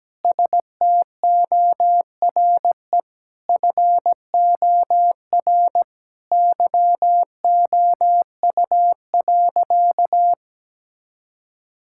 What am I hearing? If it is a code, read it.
Morse code, heard as a keyed tone, about 17 wpm, 700 Hz. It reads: STORE FOR YOU.
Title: Sentences from Top 1000 Words - Review Part 5 - 17wpm